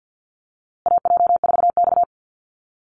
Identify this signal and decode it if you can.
Morse code, heard as a keyed tone, about 50 wpm, 700 Hz. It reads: AJ4X
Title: USA Call Signs Repeated 2x 50wpm